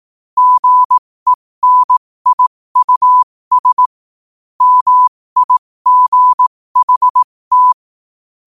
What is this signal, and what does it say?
Morse code, heard as a keyed tone, about 18 wpm, spaced as for 15 wpm, 1000 Hz. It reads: GENIUS MIGHT
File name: qotdfiles / 15wpm